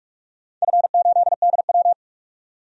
Morse code, heard as a keyed tone, about 45 wpm, 700 Hz. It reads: F8DW
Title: DXCC-100 Call Signs - 4 Characters - 45wpm